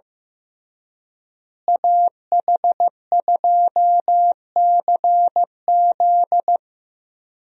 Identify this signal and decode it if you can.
Morse code, heard as a keyed tone, about 16 wpm, 700 Hz. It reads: AH2CZ
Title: USA Call Signs - 5 Characters - Repeated 3x - 15wpm